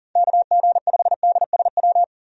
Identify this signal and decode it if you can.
Morse code, heard as a keyed tone, about 40 wpm, 700 Hz. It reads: KG5DSW
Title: USA Call Signs - 6 Characters - 40wpm